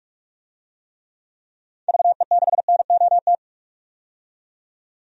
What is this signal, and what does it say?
Morse code, heard as a keyed tone, about 46 wpm, 700 Hz. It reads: VE6NOT
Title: DXCC-100 Call Signs - 6 Characters - 45wpm